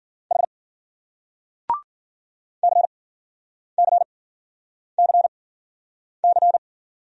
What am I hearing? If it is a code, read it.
Morse code, heard as a keyed tone, about 53 wpm, spaced as for 9 wpm, 700 Hz. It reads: H CCCC